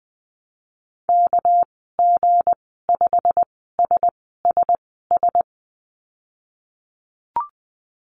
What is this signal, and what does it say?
Morse code, heard as a keyed tone, about 19 wpm, spaced as for 14 wpm, 700 Hz. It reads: KG5SSS
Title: 200 Lesson 32g - Letter k - Call Signs 20wpm